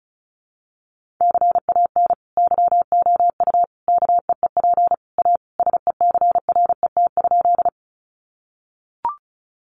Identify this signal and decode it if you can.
Morse code, heard as a keyed tone, about 34 wpm, 700 Hz. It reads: CAN YOU KEEP A SECRET?
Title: Sentences from Top 1200 Words 35wpm